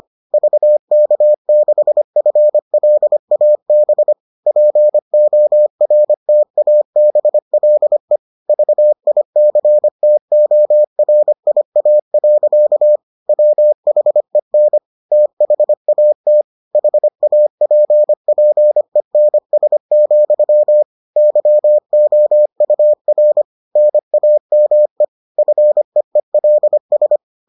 Morse code, heard as a keyed tone, about 25 wpm, 600 Hz. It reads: VK6FLAB PORTABLE VICTORIA. WHEN THAT HAPPENS, YOUR NAME FEELS